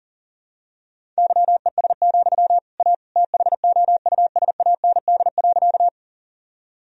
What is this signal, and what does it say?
Morse code, heard as a keyed tone, about 40 wpm, 700 Hz. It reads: YES, A THOUSAND.
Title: Sentences from Top 1000 Words - Review Part 3 - 40wpm